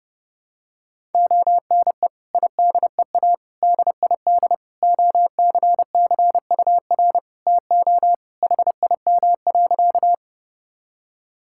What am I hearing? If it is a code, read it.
Morse code, heard as a keyed tone, about 29 wpm, 700 Hz. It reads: ONE IDEA DID OCCUR TO HIM.
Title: Sentences from Top 1000 Words - Review Part 6 - 30wpm